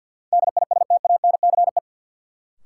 Morse code, heard as a keyed tone, about 48 wpm, 700 Hz. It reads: DISTANCE
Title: Sentences from Top 1200 Words - New Word Drill - 50wpm